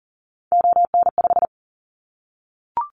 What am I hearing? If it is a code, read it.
Morse code, heard as a keyed tone, about 41 wpm, 700 Hz. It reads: ON5
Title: Callsign Prefixes for Top 100 DXCC - 40wpm